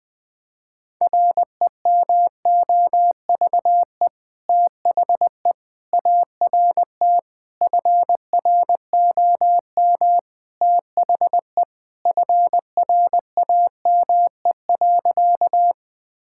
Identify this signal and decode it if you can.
Morse code, heard as a keyed tone, about 19 wpm, 700 Hz. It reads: REMOVE THE ART FROM THE FRAME.
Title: Sentences from Top 1000 Words - Review Part 6 - 20wpm